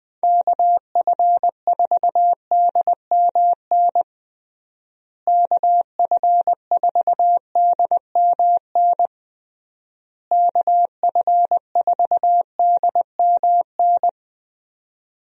Morse code, heard as a keyed tone, about 20 wpm, 700 Hz. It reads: KF4DMN KF4DMN KF4DMN
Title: USA Call Signs - 6 Characters - Repeated 3x - 20wpm